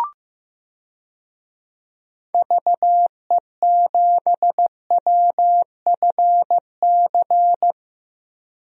Morse code, heard as a keyed tone, about 15 wpm, 700 Hz. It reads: VE7WFC
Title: DXCC-100 Call Signs - 6 Characters - Repeated 3x - 15wpm